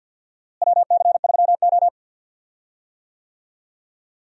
Morse code, heard as a keyed tone, about 47 wpm, 700 Hz. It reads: WC3O
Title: USA Call Signs - 4 Characters - Repeated 2x - 50wpm